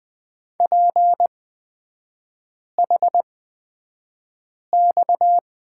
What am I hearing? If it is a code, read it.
Morse code, heard as a keyed tone, about 20 wpm, spaced as for 5 wpm, 700 Hz. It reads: PHX
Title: 229 Lesson 37e - Letter x - Three Characters 20wpm